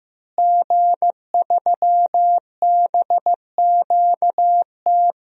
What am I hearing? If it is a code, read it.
Morse code, heard as a keyed tone, about 15 wpm, 700 Hz. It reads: G3BQT
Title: DXCC-100 Call Signs - 5 Characters - Repeated 3x - 15wpm